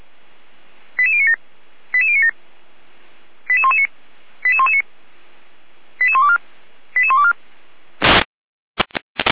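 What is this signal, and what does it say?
Five-tone selective call (ZVEI), heard as quick runs of five tones, tones of about 80 ms. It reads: 88097 88097 88119 88119 88124 88124
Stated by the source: ZVEI five-tone call received by SDR.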